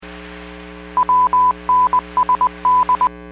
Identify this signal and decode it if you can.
Morse code, heard as a keyed tone, about 20 wpm, 1000 Hz. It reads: WNSD
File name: WPS / 01273 17-02 WPS 451.25 WPS